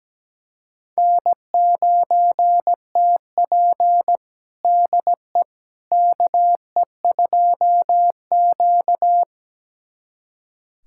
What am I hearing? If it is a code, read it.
Morse code, heard as a keyed tone, about 17 wpm, 700 Hz. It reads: N9TP DE KE2Q
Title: USA Callsign DE Callsign - 4 Characters - 17wpm